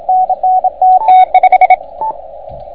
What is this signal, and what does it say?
Morse code, heard as a keyed tone, about 28 wpm, 720 Hz, with radio noise. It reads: NN TT5